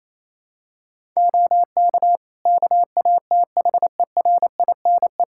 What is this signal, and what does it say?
Morse code, heard as a keyed tone, about 28 wpm, 700 Hz. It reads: OK KATHERINE
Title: Practice QSO - Exchange 3 - More Info - 28wpm